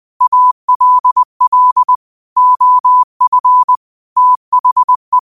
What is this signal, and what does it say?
Morse code, heard as a keyed tone, about 20 wpm, 1000 Hz. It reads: ALL OF THE
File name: qotdfiles / 20wpm